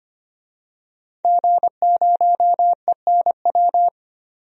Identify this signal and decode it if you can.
Morse code, heard as a keyed tone, about 25 wpm, 700 Hz. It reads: G0ENW
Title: DXCC-100 Call Signs - 5 Characters - 25wpm